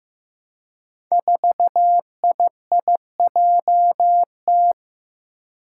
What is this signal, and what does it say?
Morse code, heard as a keyed tone, about 15 wpm, 700 Hz. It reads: 4IIJT
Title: Five Challenging Letter-Number Combinations 15wpm